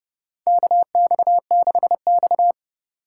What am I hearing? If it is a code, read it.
Morse code, heard as a keyed tone, about 30 wpm, 700 Hz. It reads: KX6X
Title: K1USN SST QSO Practice - 30wpm